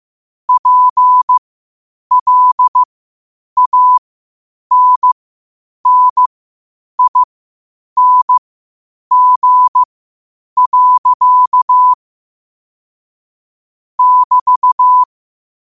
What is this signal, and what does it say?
Morse code, heard as a keyed tone, about 15 wpm, spaced as for 8 wpm, 1000 Hz. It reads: PLANNING. <BT>
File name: qotdfiles / 8wpm